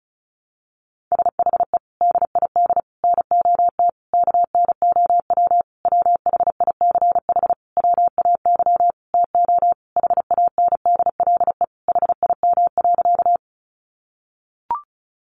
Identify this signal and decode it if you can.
Morse code, heard as a keyed tone, about 34 wpm, 700 Hz. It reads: SHE DID NOT KNOW WHICH WAY TO HANDLE HIM.
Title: Sentences from Top 1100 Words 35wpm